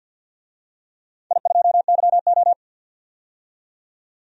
Morse code, heard as a keyed tone, about 52 wpm, 700 Hz. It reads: I2YO
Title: DXCC-100 Call Signs - 4 Characters - 50wpm